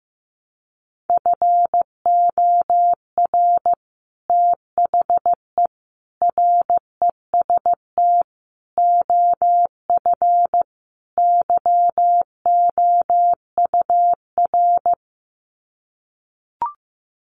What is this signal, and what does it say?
Morse code, heard as a keyed tone, about 15 wpm, 700 Hz. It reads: FOR THE REST OF YOUR
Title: Sets of 5 Words Encore 15wpm